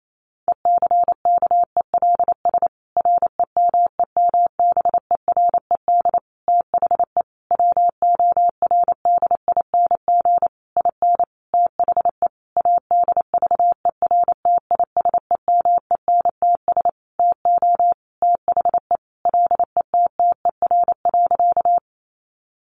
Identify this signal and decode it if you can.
Morse code, heard as a keyed tone, about 28 wpm, 700 Hz. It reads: ECKELS REMEMBERED THE WORDING IN THE ADVERTISEMENTS TO THE LETTER.
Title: A Sound Of Thunder 28wpm